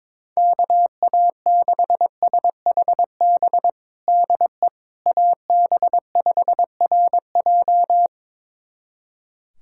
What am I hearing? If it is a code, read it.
Morse code, heard as a keyed tone, about 22 wpm, 700 Hz. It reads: KA6SHB DE AB5RJ
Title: USA Callsign DE Callsign 22wpm